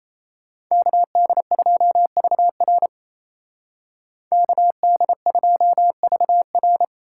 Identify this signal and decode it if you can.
Morse code, heard as a keyed tone, about 36 wpm, 700 Hz. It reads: KD2VR KD2VR